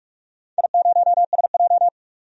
Morse code, heard as a keyed tone, about 44 wpm, 700 Hz. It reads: I0SJ
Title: DXCC-100 Call Signs - 4 Characters - 45wpm